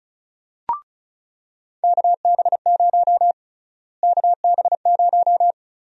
Morse code, heard as a keyed tone, about 34 wpm, 700 Hz. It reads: KB0 KB0
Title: USA Call Sign Prefixes - Repeated 2x - 35wpm